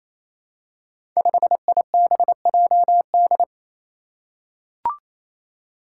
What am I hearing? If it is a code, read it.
Morse code, heard as a keyed tone, about 28 wpm, 700 Hz. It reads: 5IBJD
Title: Five Challenging Letter-Number Combinations 28wpm